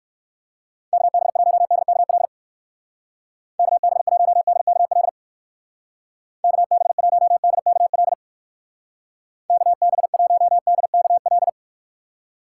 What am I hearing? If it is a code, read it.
Morse code, heard as a keyed tone, about 58 wpm, 700 Hz. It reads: KB1DKL KB1DKL KB1DKL KB1DKL